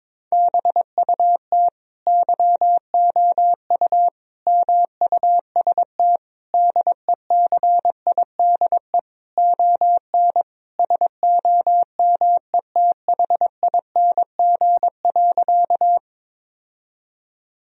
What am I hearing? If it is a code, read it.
Morse code, heard as a keyed tone, about 22 wpm, 700 Hz. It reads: BUT YOU MUST DECIDE ON SOMETHING.
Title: Sentences from Top 1000 Words - Review Part 3 - 22wpm